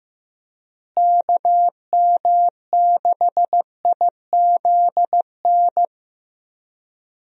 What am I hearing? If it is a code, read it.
Morse code, heard as a keyed tone, about 15 wpm, 700 Hz. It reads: KM6IZN
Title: USA Call Signs - 6 Characters - Repeated 3x - 15wpm